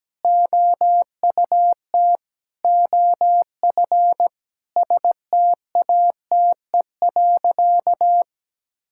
Morse code, heard as a keyed tone, about 17 wpm, 700 Hz. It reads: OUT OF STATE.
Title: Sentences from Top 1000 Words - Review Part 5 - 17wpm